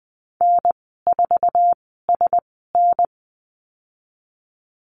Morse code, heard as a keyed tone, about 20 wpm, 700 Hz. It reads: N 4 S N
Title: 055 Lesson 10g - Letter r - Call Signs 20wpm